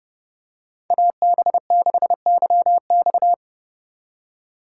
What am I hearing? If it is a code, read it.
Morse code, heard as a keyed tone, about 29 wpm, 700 Hz. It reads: AB6YX